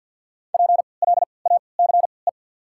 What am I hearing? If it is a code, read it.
Morse code, heard as a keyed tone, about 54 wpm, spaced as for 29 wpm, 700 Hz. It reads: PLACE
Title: 3x Character Spacing - Top 500 Words - 50wpm